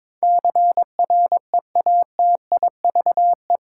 Morse code, heard as a keyed tone, about 22 wpm, 700 Hz. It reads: CREATIVE